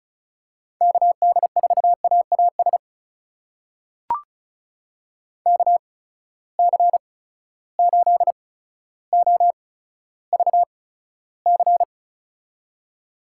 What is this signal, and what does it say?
Morse code, heard as a keyed tone, about 36 wpm, spaced as for 15 wpm, 700 Hz. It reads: KD4AAS KC8OVC
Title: USA Call Signs - 6 Characters - 8x Character Spacing - 35wpm